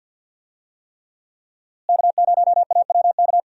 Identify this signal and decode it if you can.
Morse code, heard as a keyed tone, about 47 wpm, 700 Hz. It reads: K0AWK
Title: USA Call Signs - 5 Characters - Repeated 3x - 50wpm